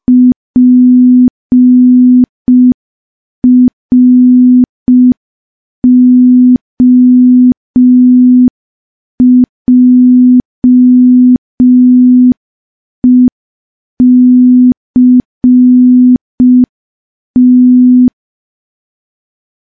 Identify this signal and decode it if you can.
Morse code, heard as a keyed tone, about 5 wpm, 250 Hz. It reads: PROJECT
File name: morse / morse-training-05WPM